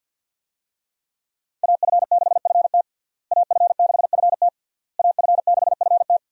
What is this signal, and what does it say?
Morse code, heard as a keyed tone, about 50 wpm, 700 Hz. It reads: AF6FT AF6FT AF6FT
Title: USA Call Signs - 5 Characters - Repeated 3x - 50wpm